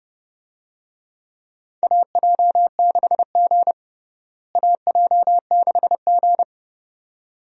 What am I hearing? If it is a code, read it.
Morse code, heard as a keyed tone, about 30 wpm, 700 Hz. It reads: AJ6G AJ6G
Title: USA Call Signs - 4 Characters - Repeated 2x - 30wpm